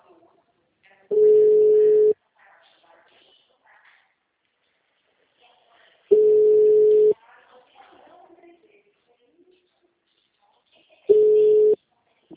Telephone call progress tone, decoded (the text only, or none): ringback tone